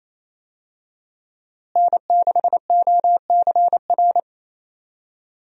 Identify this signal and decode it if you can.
Morse code, heard as a keyed tone, about 28 wpm, 700 Hz. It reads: N6OCR
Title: USA Call Signs - 5 Characters - Repeated 2x - 28wpm